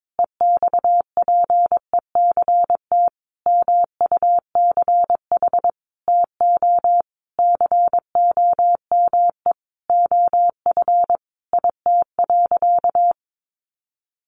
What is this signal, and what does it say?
Morse code, heard as a keyed tone, about 22 wpm, 700 Hz. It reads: EXPECT MUCH TO COME OF IT.
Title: Sentences from Top 1100 Words 22wpm